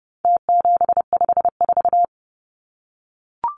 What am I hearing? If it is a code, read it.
Morse code, heard as a keyed tone, about 30 wpm, 700 Hz. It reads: T754
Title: Numbers - Contest Style - 4-Digit 30wpm